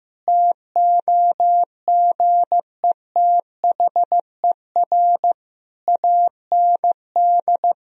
Morse code, heard as a keyed tone, about 15 wpm, 700 Hz. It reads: TOGETHER AND
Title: Sentences from Top 1000 Words - Review Part 1 - 15wpm